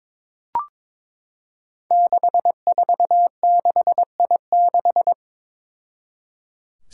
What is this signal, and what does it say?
Morse code, heard as a keyed tone, about 22 wpm, 700 Hz. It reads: 646I6
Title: Five Challenging Letter-Number Combinations 22wpm